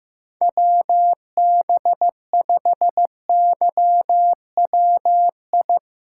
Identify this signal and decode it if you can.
Morse code, heard as a keyed tone, about 15 wpm, 700 Hz. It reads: WB5YWI
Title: USA Callsign DE Callsign - 6 Characters - 15wpm